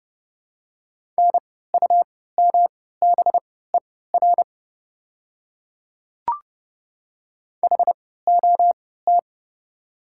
Morse code, heard as a keyed tone, about 30 wpm, spaced as for 17 wpm, 700 Hz. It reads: NUMBER HOT
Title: 3x Character Spacing - Top 100 Words - 30wpm